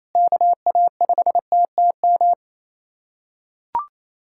Morse code, heard as a keyed tone, about 28 wpm, 700 Hz. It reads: KA5TTM
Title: USA Callsign DE Callsign 28wpm